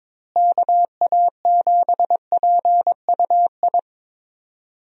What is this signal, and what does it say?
Morse code, heard as a keyed tone, about 22 wpm, 700 Hz. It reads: KA7PUI
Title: USA Call Signs - 6 Characters - Repeated 3x - 22wpm